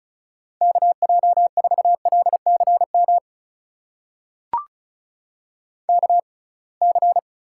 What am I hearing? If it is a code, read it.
Morse code, heard as a keyed tone, about 36 wpm, spaced as for 27 wpm, 700 Hz. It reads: KJ4LCM KC
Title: USA Call Signs - 6 Characters - 6x Character Spacing - 35wpm